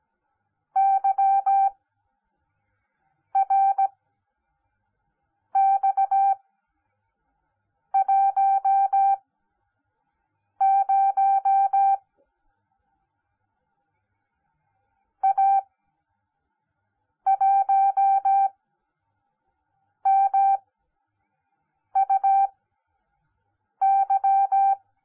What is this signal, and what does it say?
Morse code, heard as a keyed tone, about 17 wpm, spaced as for 5 wpm, 780 Hz. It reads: YRX10 A1MUY